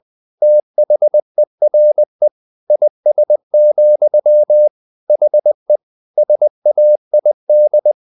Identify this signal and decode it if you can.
Morse code, heard as a keyed tone, about 20 wpm, 600 Hz. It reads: THERE IS, HE SAID
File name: the old man and the sea part 2 20wpm